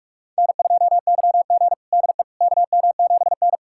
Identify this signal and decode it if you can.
Morse code, heard as a keyed tone, about 47 wpm, 700 Hz. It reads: N2YG DE KM7N
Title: USA Callsign DE Callsign - 4 Characters - 45wpm